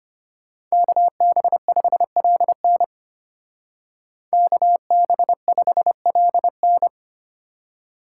Morse code, heard as a keyed tone, about 33 wpm, 700 Hz. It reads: KB5LN KB5LN